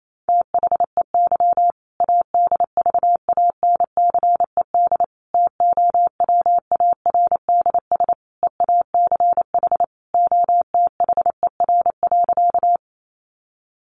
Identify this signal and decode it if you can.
Morse code, heard as a keyed tone, about 28 wpm, 700 Hz. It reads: THEY ADVANCED TOWARDS EACH OTHER.
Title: Sentences from Top 1200 Words 28wpm